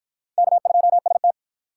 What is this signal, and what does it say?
Morse code, heard as a keyed tone, about 48 wpm, 700 Hz. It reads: K2ST